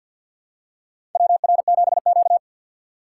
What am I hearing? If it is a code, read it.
Morse code, heard as a keyed tone, about 50 wpm, 700 Hz. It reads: WR7Q
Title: USA Call Signs - 6x Character Spacing - 50wpm